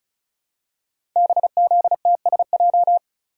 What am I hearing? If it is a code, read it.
Morse code, heard as a keyed tone, about 36 wpm, 700 Hz. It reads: BZTSJ
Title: Code Groups - 5 Letters - 35wpm